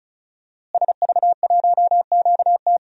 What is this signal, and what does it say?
Morse code, heard as a keyed tone, about 36 wpm, 700 Hz. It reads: SV1QT
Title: DXCC-100 Call Signs - 5 Characters - Repeated 3x - 35wpm